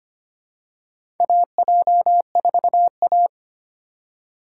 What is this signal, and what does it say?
Morse code, heard as a keyed tone, about 25 wpm, 700 Hz. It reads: AJ4A
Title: USA Call Signs - 4 Characters - 2x Character Spacing - 25wpm